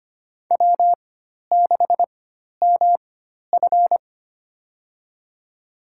Morse code, heard as a keyed tone, about 25 wpm, spaced as for 12 wpm, 700 Hz. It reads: W6MF
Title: USA Call Signs - 4 Characters - 4x Character Spacing - 25wpm